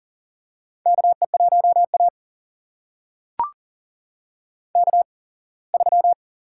Morse code, heard as a keyed tone, about 40 wpm, spaced as for 24 wpm, 700 Hz. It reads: KE1A K3
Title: USA Call Signs - 4 Characters - 8x Character Spacing - 40wpm